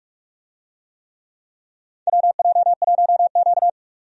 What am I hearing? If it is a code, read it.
Morse code, heard as a keyed tone, about 46 wpm, 700 Hz. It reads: WJ1Q
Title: USA Call Signs - 4 Characters - 45wpm